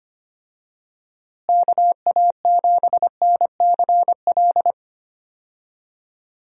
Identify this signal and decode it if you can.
Morse code, heard as a keyed tone, about 25 wpm, 700 Hz. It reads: KA7NCL
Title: USA Call Signs - 6 Characters - Repeated 3x - 25wpm